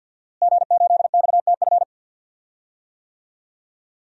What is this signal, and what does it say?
Morse code, heard as a keyed tone, about 52 wpm, 700 Hz. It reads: G8XTF
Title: DXCC-100 Call Signs - 5 Characters - Repeated 3x - 50wpm